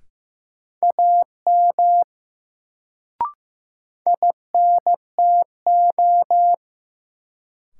Morse code, heard as a keyed tone, about 16 wpm, 700 Hz. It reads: AM INTO
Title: Sets of 2 Words - New Word Drill - 15wpm